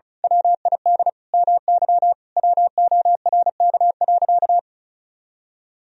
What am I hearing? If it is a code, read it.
Morse code, heard as a keyed tone, about 34 wpm, 700 Hz. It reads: WID MY WORK.
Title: Sentences from Top 300 Words and Abbreviations 35wpm